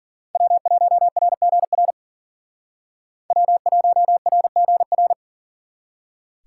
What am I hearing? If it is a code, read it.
Morse code, heard as a keyed tone, about 52 wpm, 700 Hz. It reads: W1RGR W1RGR